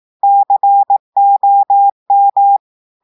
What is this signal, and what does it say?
Morse code, heard as a keyed tone, about 18 wpm, 800 Hz. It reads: COM
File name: audios / portugais / portugais texte en clair 2 vitesse 18 mots minutes